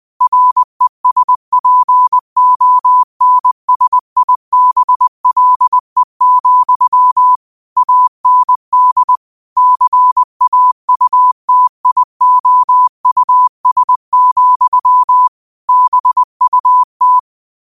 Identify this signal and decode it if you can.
Morse code, heard as a keyed tone, about 20 wpm, 1000 Hz. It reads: RESPONSIBLE, AND CAUTIOUS, BUT